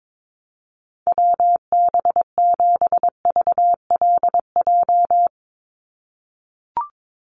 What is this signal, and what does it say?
Morse code, heard as a keyed tone, about 22 wpm, 700 Hz. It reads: WB7VLJ
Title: USA Call Signs 22wpm